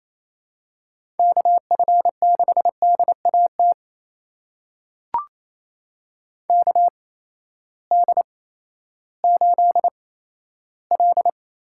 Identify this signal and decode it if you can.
Morse code, heard as a keyed tone, about 27 wpm, spaced as for 14 wpm, 700 Hz. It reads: KF6DAT KD8L
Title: USA Call Signs - 6 Characters - 8x Character Spacing - 28wpm